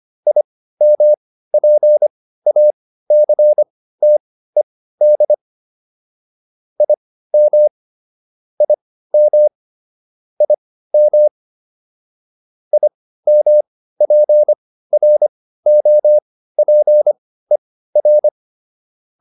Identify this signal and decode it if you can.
Morse code, heard as a keyed tone, about 25 wpm, spaced as for 15 wpm, 600 Hz. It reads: IMPACTED IM IM IM IMPROPER